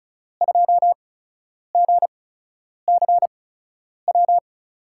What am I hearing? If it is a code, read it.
Morse code, heard as a keyed tone, about 34 wpm, spaced as for 10 wpm, 700 Hz. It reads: 2GCW
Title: USA Call Signs - 6 Characters - 8x Character Spacing - 35wpm